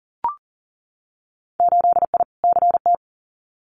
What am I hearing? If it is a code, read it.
Morse code, heard as a keyed tone, about 40 wpm, 700 Hz. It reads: 8I CT
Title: Winter Field Day Exchanges Encore 40wpm